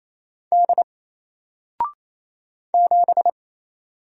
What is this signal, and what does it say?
Morse code, heard as a keyed tone, about 28 wpm, 700 Hz. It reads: D 7
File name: Single-Letter-Number-28wpm